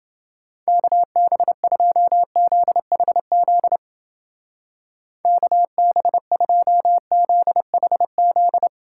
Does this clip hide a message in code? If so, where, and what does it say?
Yes: Morse code from 0.7 to 8.7 s, heard as a keyed tone, about 29 wpm, 700 Hz. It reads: KB2ZHZ KB2ZHZ